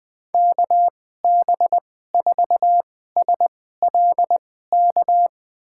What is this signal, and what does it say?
Morse code, heard as a keyed tone, about 20 wpm, spaced as for 14 wpm, 700 Hz. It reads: KB4SLK
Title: USA Call Signs - 6 Characters - 2x Character Spacing - 20wpm